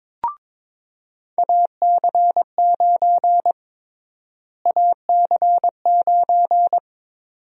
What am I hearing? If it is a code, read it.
Morse code, heard as a keyed tone, about 23 wpm, 700 Hz. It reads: AC9 AC9
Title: USA Call Sign Prefixes - Repeated 2x - 22wpm